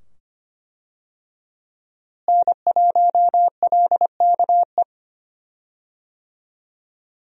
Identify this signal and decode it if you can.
Morse code, heard as a keyed tone, about 25 wpm, 700 Hz. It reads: N1LKE
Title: USA Call Signs - 5 Characters - Repeated 2x - 25wpm